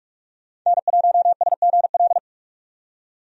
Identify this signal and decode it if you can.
Morse code, heard as a keyed tone, about 42 wpm, 700 Hz. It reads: N1SGL
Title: USA Call Signs - 5 Characters - 45wpm